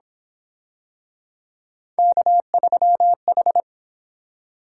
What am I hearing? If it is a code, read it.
Morse code, heard as a keyed tone, about 27 wpm, 700 Hz. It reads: K3H
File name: Speed-Racing - Three Letter-Number Combinations - 22wpm